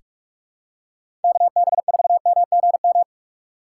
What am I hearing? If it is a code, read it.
Morse code, heard as a keyed tone, about 47 wpm, 700 Hz. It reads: KB4MGM
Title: USA Call Signs - 6 Characters - 3x Character Spacing - 45wpm